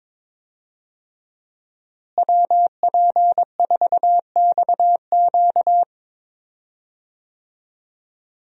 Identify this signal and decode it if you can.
Morse code, heard as a keyed tone, about 21 wpm, 700 Hz. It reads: WP4XQ
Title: USA Call Signs - 5 Characters - Repeated 2x - 22wpm